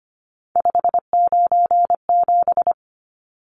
Morse code, heard as a keyed tone, about 25 wpm, 700 Hz. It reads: 597
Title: Numbers - 3-Digit 25wpm